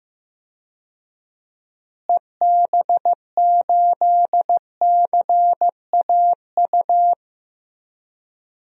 Morse code, heard as a keyed tone, about 15 wpm, 700 Hz. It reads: EB8CAU
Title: DXCC-100 Call Signs - 6 Characters - Repeated 3x - 15wpm